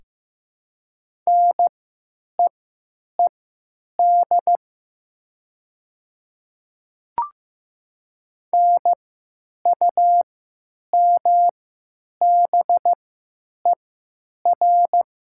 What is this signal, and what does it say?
Morse code, heard as a keyed tone, about 15 wpm, spaced as for 9 wpm, 700 Hz. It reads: NEED NUMBER
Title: 3x Character Spacing - Top 200 Words - 15wpm